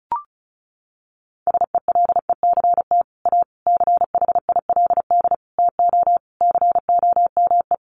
Morse code, heard as a keyed tone, about 34 wpm, 700 Hz. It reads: SELECT A CHILD TO COME
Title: Sentences from Top 1300 Words 35wpm